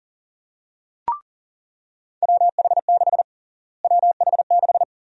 Morse code, heard as a keyed tone, about 40 wpm, 700 Hz. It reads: WH6 WH6
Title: USA Call Sign Prefixes - Repeated 2x - 40wpm